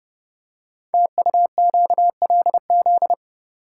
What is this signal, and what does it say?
Morse code, heard as a keyed tone, about 28 wpm, 700 Hz. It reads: TUQLZ
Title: Code Groups - 5 Letters - 30wpm